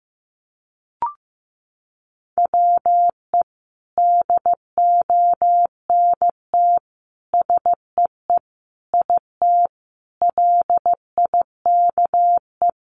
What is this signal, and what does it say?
Morse code, heard as a keyed tone, about 15 wpm, 700 Hz. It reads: WE DONT SEE IT LIKE
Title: Sentences from Top 300 Words 15wpm